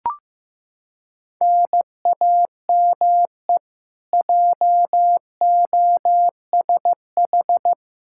Morse code, heard as a keyed tone, about 16 wpm, 700 Hz. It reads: NAME JOSH
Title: K1USN SST QSO Practice 15wpm